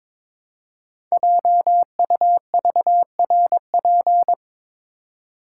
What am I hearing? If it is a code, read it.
Morse code, heard as a keyed tone, about 22 wpm, 700 Hz. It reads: JUVRP
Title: Code Groups - 5 Letters - 22wpm